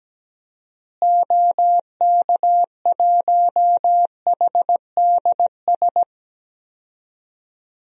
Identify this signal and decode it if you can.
Morse code, heard as a keyed tone, about 17 wpm, 700 Hz. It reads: OK1HDS
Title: DXCC-100 Call Signs - 6 Characters - Repeated 3x - 17wpm